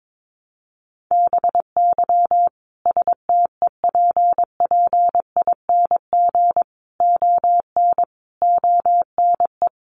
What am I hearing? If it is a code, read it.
Morse code, heard as a keyed tone, about 22 wpm, 700 Hz. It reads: BY STEPPING ON ONE
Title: A Sound Of Thunder 22wpm